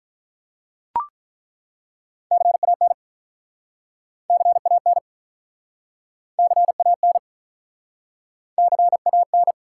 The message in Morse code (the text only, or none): CAN CAN CAN CAN